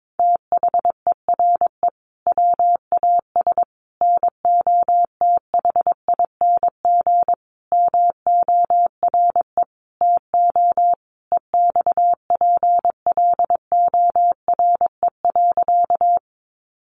Morse code, heard as a keyed tone, about 22 wpm, 700 Hz. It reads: THERE WAS NOTHING MORE TO EXPLORE.
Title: Sentences from Top 1100 Words 22wpm